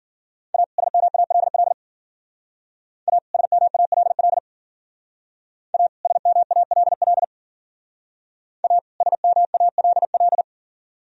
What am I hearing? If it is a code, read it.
Morse code, heard as a keyed tone, about 51 wpm, 700 Hz. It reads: A SMALL A SMALL A SMALL A SMALL